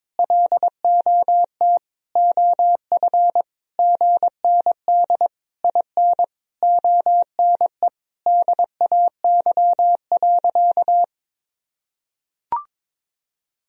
Morse code, heard as a keyed tone, about 22 wpm, 700 Hz. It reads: LOT OF GND IN ONE DAY.
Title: Sentences from Top 500 Words and Abbreviations 22wpm